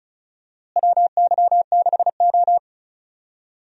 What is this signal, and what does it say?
Morse code, heard as a keyed tone, about 34 wpm, 700 Hz. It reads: WY6O
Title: USA Call Signs - 4 Characters - Repeated 2x - 35wpm